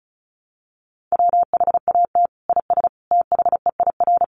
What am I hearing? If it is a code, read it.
Morse code, heard as a keyed tone, about 34 wpm, 700 Hz. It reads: WHAT IS THEIR
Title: Sentences from Top 1200 Words 35wpm